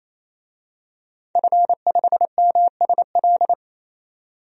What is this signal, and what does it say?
Morse code, heard as a keyed tone, about 27 wpm, 700 Hz. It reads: F5MSL
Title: DXCC-100 Call Signs - 5 Characters - 28wpm